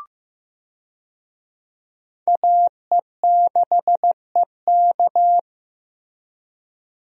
Morse code, heard as a keyed tone, about 15 wpm, 700 Hz. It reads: AE6EK
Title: USA Call Signs - 5 Characters - Repeated 3x - 15wpm